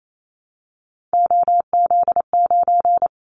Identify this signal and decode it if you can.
Morse code, heard as a keyed tone, about 28 wpm, 700 Hz. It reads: OZ9
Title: Callsign Prefixes for Top 50 DXCC - 28wpm